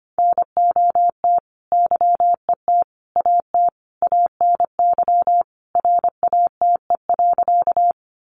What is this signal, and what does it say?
Morse code, heard as a keyed tone, about 25 wpm, 700 Hz. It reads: NOT YET AT ANY RATE.
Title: Sentences from Top 700 Words 25wpm